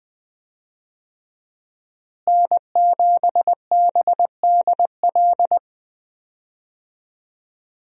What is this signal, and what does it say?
Morse code, heard as a keyed tone, about 20 wpm, 700 Hz. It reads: N7BDL
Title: USA Call Signs - 5 Characters - Repeated 3x - 20wpm